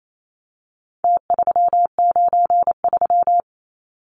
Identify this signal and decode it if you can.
Morse code, heard as a keyed tone, about 28 wpm, 700 Hz. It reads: T393
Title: Numbers - Contest Style - 4-Digit 28wpm